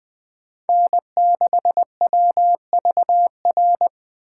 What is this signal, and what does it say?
Morse code, heard as a keyed tone, about 20 wpm, 700 Hz. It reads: N6WVR
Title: Speed Racing - USA Call Signs 17wpm